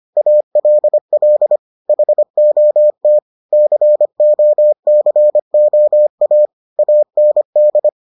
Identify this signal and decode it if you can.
Morse code, heard as a keyed tone, about 25 wpm, 600 Hz. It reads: ALL HOT COCOA AND